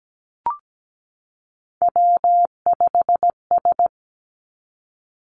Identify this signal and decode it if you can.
Morse code, heard as a keyed tone, about 17 wpm, 700 Hz. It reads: W5S
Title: Three Challenging Letter-Number Combinations 17wpm